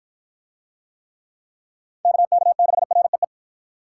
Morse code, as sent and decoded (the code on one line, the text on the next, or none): -.- -.- -.... .-. .
KK6REE